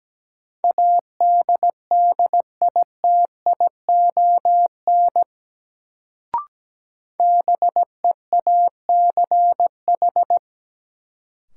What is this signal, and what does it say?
Morse code, heard as a keyed tone, about 17 wpm, 700 Hz. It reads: ADDITION BEACH
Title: Sentences from Top 1500 Words - New Word Drill Alt - 17wpm